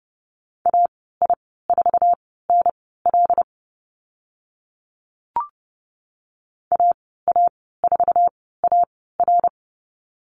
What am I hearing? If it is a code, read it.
Morse code, heard as a keyed tone, about 30 wpm, spaced as for 17 wpm, 700 Hz. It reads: AI4NL AA4AR
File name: Lesson 14g - Number 2 - Call Signs 30wpm